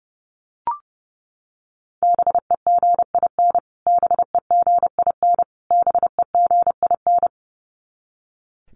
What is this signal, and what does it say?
Morse code, heard as a keyed tone, about 30 wpm, 700 Hz. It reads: BEGIN BEGIN BEGIN
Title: Top 300 Words Repeated 30wpm